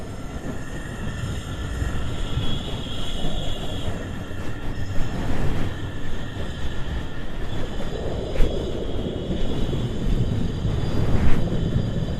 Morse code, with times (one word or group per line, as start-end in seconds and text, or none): none